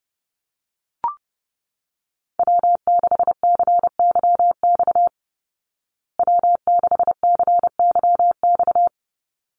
Morse code, heard as a keyed tone, about 30 wpm, 700 Hz. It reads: W6CYX W6CYX
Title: Call Signs Repeated 3x - 30wpm